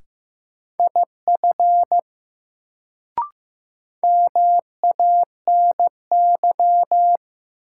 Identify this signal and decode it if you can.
Morse code, heard as a keyed tone, about 15 wpm, 700 Hz. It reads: IF MANY
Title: Sentences from Top 100 Words - New Word Drill - 15wpm